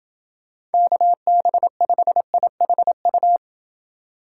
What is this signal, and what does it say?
Morse code, heard as a keyed tone, about 28 wpm, 700 Hz. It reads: KB5IHU